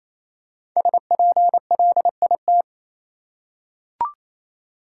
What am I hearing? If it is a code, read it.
Morse code, heard as a keyed tone, about 28 wpm, 700 Hz. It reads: SPLIT
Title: Five Letter Words - 28wpm